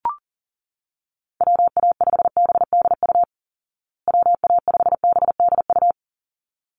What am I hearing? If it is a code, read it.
Morse code, heard as a keyed tone, about 40 wpm, 700 Hz. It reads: WA5BDU WA5BDU
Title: Call Signs Repeated 3x - 40wpm